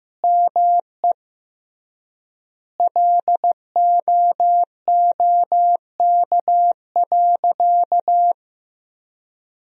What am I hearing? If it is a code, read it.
Morse code, heard as a keyed tone, about 15 wpm, 700 Hz. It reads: ME LOOK.
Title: 3x Word-Spacing - Sentences from Top 300 Words - 15wpm